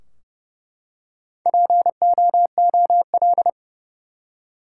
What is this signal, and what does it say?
Morse code, heard as a keyed tone, about 30 wpm, 700 Hz. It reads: POOL
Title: Four Letter Words - 30wpm